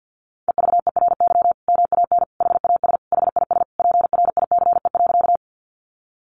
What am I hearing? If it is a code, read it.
Morse code, heard as a keyed tone, about 50 wpm, 700 Hz. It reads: EVERY MAN HAS HIS PRICE.
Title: Sentences from Top 600 Words 50wpm